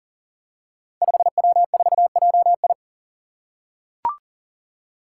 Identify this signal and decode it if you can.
Morse code, heard as a keyed tone, about 40 wpm, 700 Hz. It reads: 5W4JI
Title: Five Challenging Letter-Number Combinations 40wpm